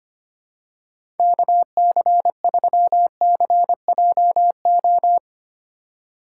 Morse code, heard as a keyed tone, about 25 wpm, 700 Hz. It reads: KC3CJO
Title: USA Call Signs - 2x Character Spacing - 25wpm